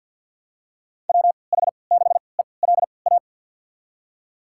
Morse code, heard as a keyed tone, about 51 wpm, spaced as for 28 wpm, 700 Hz. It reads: WH6ELA
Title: USA Call Signs - 6 Characters - 3x Character Spacing - 50wpm